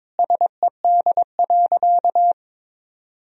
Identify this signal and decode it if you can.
Morse code, heard as a keyed tone, about 21 wpm, 700 Hz. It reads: SED.
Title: Sentences from Top 100 Words and Abbreviations 22wpm